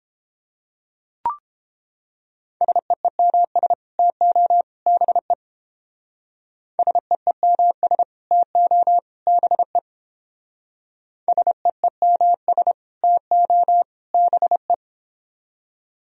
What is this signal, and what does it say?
Morse code, heard as a keyed tone, about 29 wpm, 700 Hz. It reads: SEEMS TO BE SEEMS TO BE SEEMS TO BE